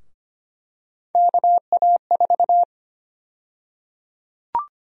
Morse code, heard as a keyed tone, about 25 wpm, 700 Hz. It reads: KA4
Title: USA Call Sign Prefixes - Repeated 2x - 25wpm